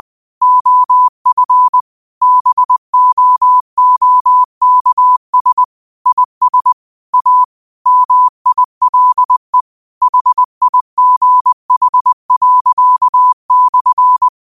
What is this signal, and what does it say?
Morse code, heard as a keyed tone, about 20 wpm, 1000 Hz. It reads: OF BOOKS IS A MILE HIGH./
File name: qotdfiles / 20wpm